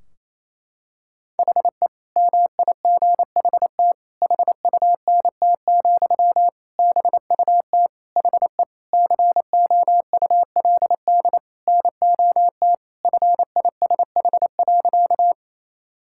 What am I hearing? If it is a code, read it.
Morse code, heard as a keyed tone, about 28 wpm, 700 Hz. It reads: HE MIGHT HUNT, BUT HE COULD NOT FISH.
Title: Sentences from Top 1600 Words 28wpm